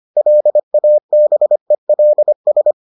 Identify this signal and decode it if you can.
Morse code, heard as a keyed tone, about 25 wpm, 600 Hz. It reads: LABELS